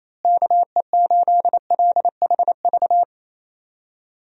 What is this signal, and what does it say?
Morse code, heard as a keyed tone, about 27 wpm, 700 Hz. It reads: KE8LHV
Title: USA Call Signs - 6 Characters - Repeated 3x - 28wpm